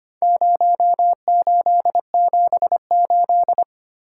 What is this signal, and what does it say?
Morse code, heard as a keyed tone, about 25 wpm, 700 Hz. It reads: 0878
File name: practice / 25wpm / Four Letter-Number Combinations - 25wpm